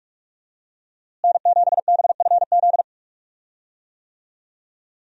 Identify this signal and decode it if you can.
Morse code, heard as a keyed tone, about 47 wpm, 700 Hz. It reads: N7BFZ
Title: USA Call Signs - 5 Characters - Repeated 2x - 45wpm